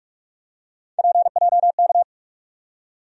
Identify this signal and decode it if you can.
Morse code, heard as a keyed tone, about 41 wpm, 700 Hz. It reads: PJK
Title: USA Call Sign Postfixes - 3 Letters 45wpm